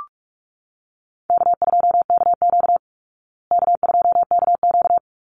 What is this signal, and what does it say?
Morse code, heard as a keyed tone, about 44 wpm, 700 Hz. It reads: K2KQ K2KQ
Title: Call Signs Repeated 3x - 45wpm